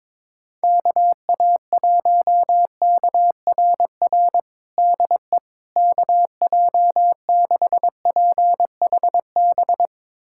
Morse code, heard as a keyed tone, about 22 wpm, 700 Hz. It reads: KA1KRR DE KJ6PHB
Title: USA Callsign DE Callsign - 6 Characters - 22wpm